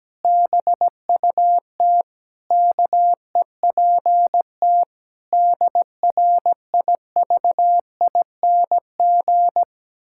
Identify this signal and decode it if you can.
Morse code, heard as a keyed tone, about 17 wpm, 700 Hz. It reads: BUT KEPT DRIVING
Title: Pangram Challenge v2 - 17wpm